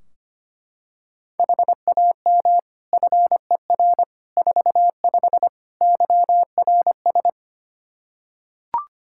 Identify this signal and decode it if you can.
Morse code, heard as a keyed tone, about 25 wpm, 700 Hz. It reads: HAM FER 45 YRS
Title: K1USN SST QSO Practice 25wpm